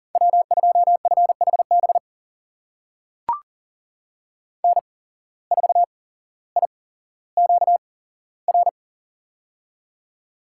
Morse code, heard as a keyed tone, about 38 wpm, spaced as for 17 wpm, 700 Hz. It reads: W2FHB N4IQR
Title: USA Call Signs - 5 Characters - 8x Character Spacing - 40wpm